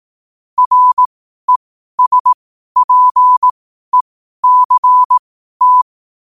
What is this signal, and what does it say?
Morse code, heard as a keyed tone, about 18 wpm, spaced as for 12 wpm, 1000 Hz. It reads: RESPECT
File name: qotdfiles / 12wpm